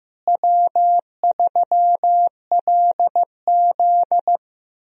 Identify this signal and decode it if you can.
Morse code, heard as a keyed tone, about 15 wpm, 700 Hz. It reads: W3LZ
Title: USA Call Signs - 4 Characters - Repeated 3x - 15wpm